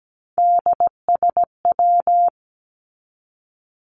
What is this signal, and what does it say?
Morse code, heard as a keyed tone, about 17 wpm, 700 Hz. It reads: DSW
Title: Abbreviations 17wpm